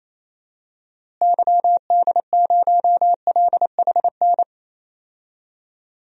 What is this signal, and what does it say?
Morse code, heard as a keyed tone, about 28 wpm, 700 Hz. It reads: YD0LHN
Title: DXCC-100 Call Signs - 6 Characters - Repeated 2x - 28wpm